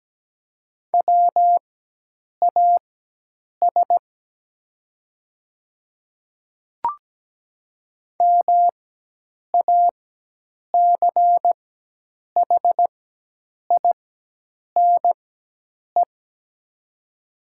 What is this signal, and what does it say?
Morse code, heard as a keyed tone, about 17 wpm, spaced as for 8 wpm, 700 Hz. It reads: WAS MACHINE
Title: 4x Character Spacing - Top 500 Words - 17wpm